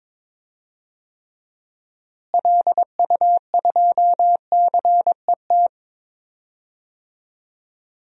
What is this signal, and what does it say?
Morse code, heard as a keyed tone, about 22 wpm, 700 Hz. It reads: LU2CET
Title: DXCC-100 Call Signs - 6 Characters - Repeated 2x - 22wpm